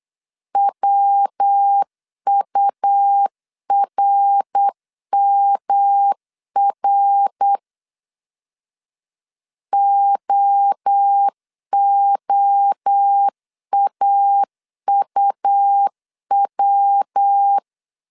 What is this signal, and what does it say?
Morse code, heard as a keyed tone, about 8 wpm, 800 Hz. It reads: WURMR OOAUW
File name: CW / DET / 1 3 1500C S B 8wpm 4141